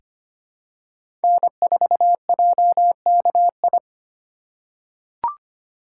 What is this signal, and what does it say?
Morse code, heard as a keyed tone, about 25 wpm, 700 Hz. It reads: N4JKI
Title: USA Call Signs - 5 Characters - 3x Character Spacing - 25wpm